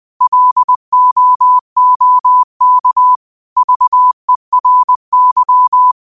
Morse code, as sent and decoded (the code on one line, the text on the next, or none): .-.. --- --- -.- / ...- . .-. -.--
LOOK VERY